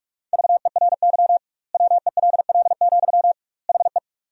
Morse code, heard as a keyed tone, about 46 wpm, 700 Hz. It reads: VERY WELL, HE
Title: 2x Word-Spacing - Sentences from Top 200 Words - 45wpm